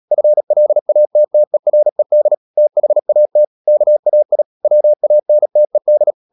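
Morse code, heard as a keyed tone, about 37 wpm, 600 Hz. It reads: FLATTERED THAT KAI WANTED
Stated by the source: Morse code practice at 35–39 wpm sending English text of Ken Follett's novel Never, from CW med Gnister.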